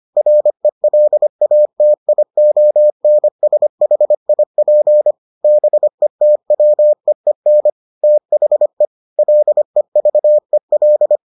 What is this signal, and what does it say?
Morse code, heard as a keyed tone, about 25 wpm, 600 Hz. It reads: RELATIONSHIP BETWEEN THE LEVEL